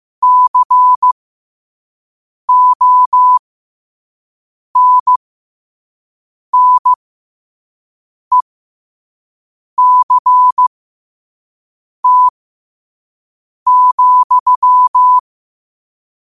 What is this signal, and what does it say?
Morse code, heard as a keyed tone, about 15 wpm, spaced as for 5 wpm, 1000 Hz. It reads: CONNECT,
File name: qotdfiles / 5wpm